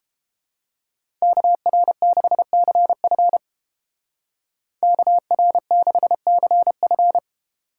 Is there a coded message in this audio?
Yes: Morse code from 1.2 to 7.2 s, heard as a keyed tone, about 31 wpm, 700 Hz. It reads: KR6CF KR6CF